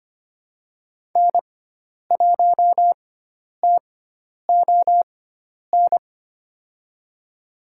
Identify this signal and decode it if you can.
Morse code, heard as a keyed tone, about 25 wpm, spaced as for 10 wpm, 700 Hz. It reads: N1TON
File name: Lesson 16g - Letter c - Call Signs 25wpm